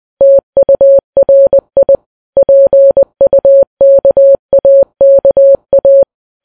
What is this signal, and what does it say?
Morse code, heard as a keyed tone, about 20 wpm, 550 Hz. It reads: TURI PUKAKA